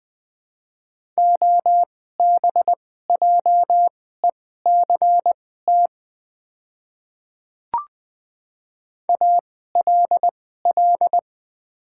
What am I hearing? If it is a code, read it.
Morse code, heard as a keyed tone, about 20 wpm, spaced as for 15 wpm, 700 Hz. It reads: OBJECT ALL
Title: 2x Character Spacing - Top 500 Words - 20wpm